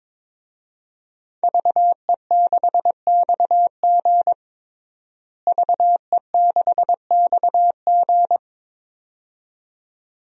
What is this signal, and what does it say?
Morse code, heard as a keyed tone, about 22 wpm, 700 Hz. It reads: VE6XG VE6XG
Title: DXCC-100 Call Signs - 5 Characters - Repeated 2x - 22wpm